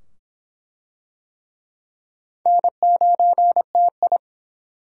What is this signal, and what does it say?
Morse code, heard as a keyed tone, about 24 wpm, 700 Hz. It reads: N9TI